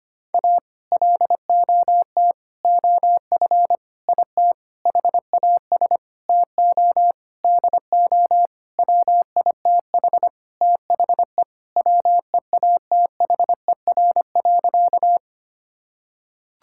Morse code, heard as a keyed tone, about 25 wpm, 700 Hz. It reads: A LOT OF IT HAS TO DO WITH THE WEATHER.
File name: practice / 25wpm / Sentences with 9-Word Phrases - 25wpm